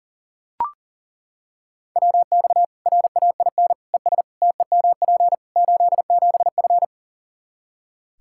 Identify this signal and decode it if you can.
Morse code, heard as a keyed tone, about 37 wpm, 700 Hz. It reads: WX RAIN ES TEMP 87F
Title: Practice QSO - Exchange 2 - Info - 40wpm